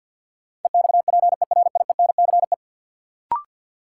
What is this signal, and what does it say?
Morse code, heard as a keyed tone, about 52 wpm, 700 Hz. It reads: EXPERIENCE
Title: Sentences from Top 600 Words - New Word Drill - 50wpm